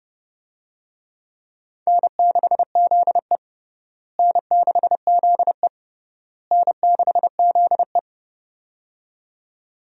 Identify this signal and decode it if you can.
Morse code, heard as a keyed tone, about 30 wpm, 700 Hz. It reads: N6ZE N6ZE N6ZE
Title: USA Call Signs - 4 Characters - Repeated 3x - 30wpm